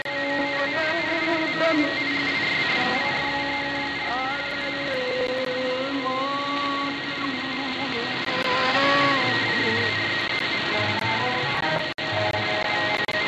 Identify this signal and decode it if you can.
Two-tone teleprinter signal, about 200 Hz shift.